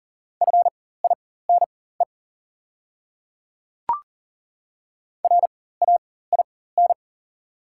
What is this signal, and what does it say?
Morse code, heard as a keyed tone, about 40 wpm, spaced as for 19 wpm, 690 Hz. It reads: FINE RAIN
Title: 4x Character Spacing - Top 500 Words - 40wpm